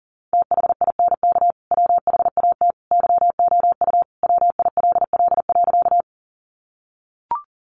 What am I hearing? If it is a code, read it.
Morse code, heard as a keyed tone, about 40 wpm, 700 Hz. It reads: THINK WHAT YOU WILL.
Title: Sentences from Top 200 Words 40wpm